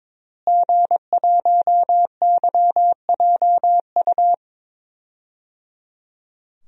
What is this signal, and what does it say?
Morse code, heard as a keyed tone, about 22 wpm, 700 Hz. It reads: G1YJU
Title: DXCC-100 Call Signs - 5 Characters - Repeated 3x - 22wpm